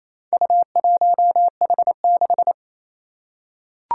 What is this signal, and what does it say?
Morse code, heard as a keyed tone, about 28 wpm, 700 Hz. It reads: U1H6
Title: Four Challenging Letter-Number Combinations 28wpm